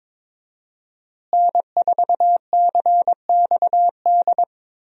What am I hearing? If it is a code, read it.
Morse code, heard as a keyed tone, about 22 wpm, 700 Hz. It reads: N4CXD
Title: USA Call Signs - 5 Characters - Repeated 2x - 22wpm